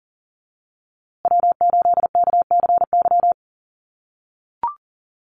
Morse code, heard as a keyed tone, about 40 wpm, 700 Hz. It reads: W8KCY
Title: USA Call Signs Repeated 2x 40wpm